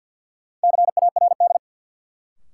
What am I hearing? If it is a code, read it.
Morse code, heard as a keyed tone, about 52 wpm, 700 Hz. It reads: CARD
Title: Sentences from Top 600 Words - New Word Drill - 50wpm